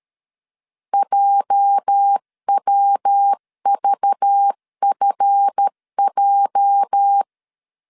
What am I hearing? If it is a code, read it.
Morse code, heard as a keyed tone, about 13 wpm, 800 Hz. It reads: JWVFJ
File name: LEZIONI / 9 28 12wpm 2351